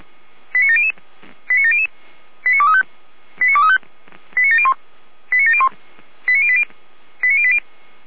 Five-tone selective call (ZVEI), heard as quick runs of five tones, tones of about 70 ms. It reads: 89700 89700 89235 89235 89701 89701 89080 89080